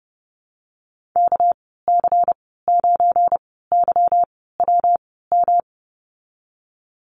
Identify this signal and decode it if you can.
Morse code, heard as a keyed tone, about 31 wpm, spaced as for 17 wpm, 700 Hz. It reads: KC9YWM
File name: Lesson 37g - Letter x - Call Signs 30wpm